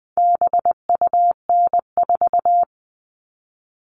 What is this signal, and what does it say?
Morse code, heard as a keyed tone, about 20 wpm, 700 Hz. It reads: BUN4
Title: All Cut Numbers - 4-Digit 20wpm